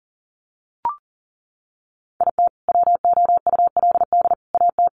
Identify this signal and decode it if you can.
Morse code, heard as a keyed tone, about 40 wpm, 700 Hz. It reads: IT WOULD AT
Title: Sentences from Top 700 Words 40wpm